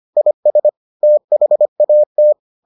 Morse code, heard as a keyed tone, about 25 wpm, 600 Hz. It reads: IS THAT